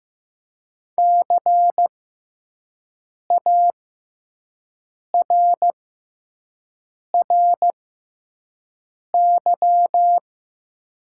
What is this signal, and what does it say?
Morse code, heard as a keyed tone, about 15 wpm, spaced as for 5 wpm, 700 Hz. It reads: CARRY
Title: 6x Character Spacing - Top 300 Words - 15wpm